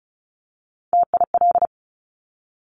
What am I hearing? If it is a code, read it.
Morse code, heard as a keyed tone, about 36 wpm, 700 Hz. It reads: TIL
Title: Top Three Letter Combinations in Words 35wpm